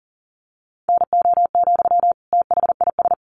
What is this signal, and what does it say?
Morse code, heard as a keyed tone, about 40 wpm, 700 Hz. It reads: NO, THIS
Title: Sentences from Top 1100 Words 40wpm